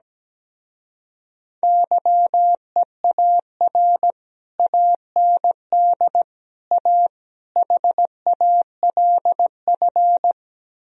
Morse code, heard as a keyed tone, about 17 wpm, 700 Hz. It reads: YEAR AND A HALF
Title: Speed-Racing - Sets of 4 Words 17wpm